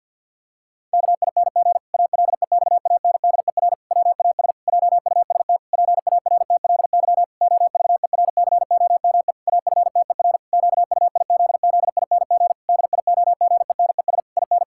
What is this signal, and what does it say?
Morse code, heard as a keyed tone, about 48 wpm, 700 Hz. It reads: KING ALEXANDER WAS JUST PARTLY OVERCOME AFTER QUIZZING DIOGENES IN
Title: Pangram Challenge v2 - 50wpm